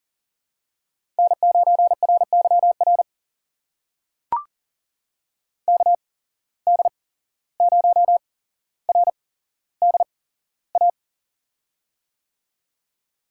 Morse code, heard as a keyed tone, about 38 wpm, spaced as for 16 wpm, 700 Hz. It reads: N9RYR KD0RDA
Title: USA Call Signs - 8x Character Spacing - 40wpm